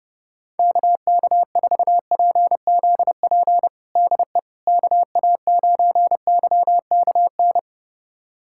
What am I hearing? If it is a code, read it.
Morse code, heard as a keyed tone, about 32 wpm, 700 Hz. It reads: KK4PZP DE KA9YKN
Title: USA Callsign DE Callsign - 6 Characters - 30wpm